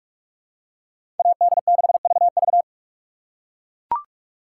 Morse code, heard as a keyed tone, about 43 wpm, 700 Hz. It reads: AD6VV
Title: USA Call Signs - 5 Characters - 6x Character Spacing - 45wpm